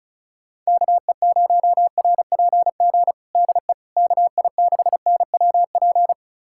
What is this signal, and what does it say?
Morse code, heard as a keyed tone, about 36 wpm, 700 Hz. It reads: KE0RPG DE KI6NWP
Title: USA Callsign DE Callsign - 6 Characters - 35wpm